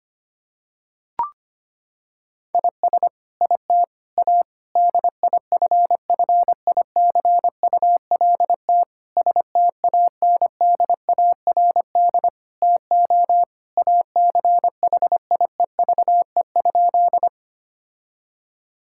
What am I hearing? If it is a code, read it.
Morse code, heard as a keyed tone, about 25 wpm, 700 Hz. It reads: IS IT A DIFFICULT STANDARD TO ACHIEVE?
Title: Sentences from Top 1000 Words - Review Part 6 - 25wpm